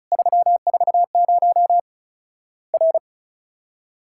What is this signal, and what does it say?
Morse code, heard as a keyed tone, about 36 wpm, 700 Hz. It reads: 340 R
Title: Worked All Europe DX Contest - QTC Exchange - 35wpm